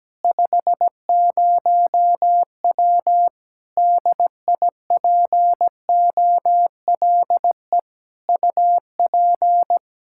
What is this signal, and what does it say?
Morse code, heard as a keyed tone, about 17 wpm, 700 Hz. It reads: 50W DIPOLE UP
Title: Practice QSO - Exchange 2 - Info - 17wpm